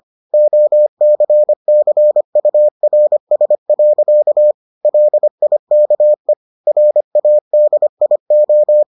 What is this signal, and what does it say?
Morse code, heard as a keyed tone, about 25 wpm, 600 Hz. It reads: OCCURS. LIKE RADIO